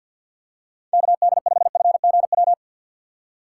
Morse code, heard as a keyed tone, about 53 wpm, 700 Hz. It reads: KD5FGW
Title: USA Call Signs - 4x Character Spacing - 50wpm